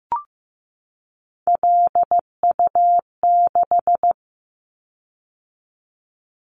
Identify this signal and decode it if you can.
Morse code, heard as a keyed tone, about 15 wpm, 700 Hz. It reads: LU6
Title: Callsign Prefixes for Top 50 DXCC - 15wpm